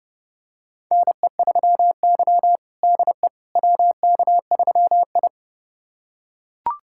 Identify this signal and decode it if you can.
Morse code, heard as a keyed tone, about 30 wpm, 700 Hz. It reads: NE3Y DE WK3I
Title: USA Callsign DE Callsign - 4 Characters - 30wpm